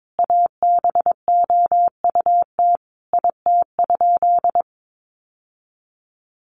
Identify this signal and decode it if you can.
Morse code, heard as a keyed tone, about 22 wpm, 700 Hz. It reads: ABOUT IT?
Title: Sentences from Top 1000 Words 22wpm